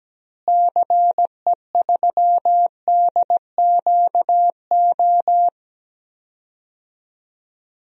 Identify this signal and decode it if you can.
Morse code, heard as a keyed tone, about 17 wpm, 700 Hz. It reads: CE3DQO
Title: DXCC-100 Call Signs - 6 Characters - Repeated 2x - 17wpm